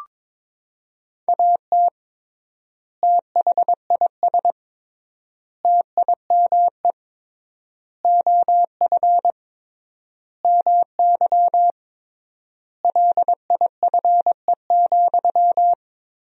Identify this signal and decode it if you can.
Morse code, heard as a keyed tone, about 23 wpm, 700 Hz. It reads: AT THIS TIME OF MY LIFE,
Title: 3x Word-Spacing - Sentences from Top 300 Words - 22wpm